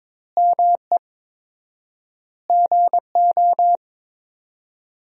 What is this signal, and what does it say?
Morse code, heard as a keyed tone, about 22 wpm, 700 Hz. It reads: ME GO
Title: 4x Word-Spacing - Sentences from Top 300 Words - 22wpm